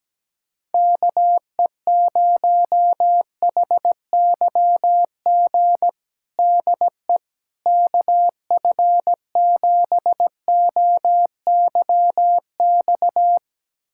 Morse code, heard as a keyed tone, about 17 wpm, 700 Hz. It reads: KE0HYG DE KF7OYX
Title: USA Callsign DE Callsign - 6 Characters - 17wpm